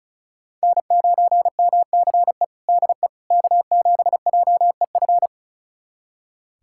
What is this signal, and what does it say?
Morse code, heard as a keyed tone, about 36 wpm, 700 Hz. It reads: N9MCE DE K7JEF
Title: USA Callsign DE Callsign - 5 Characters - 35wpm